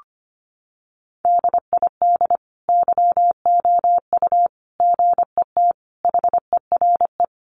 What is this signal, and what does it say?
Morse code, heard as a keyed tone, about 25 wpm, 700 Hz. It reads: DID YOU GET HERE
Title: Sentences from Top 700 Words 25wpm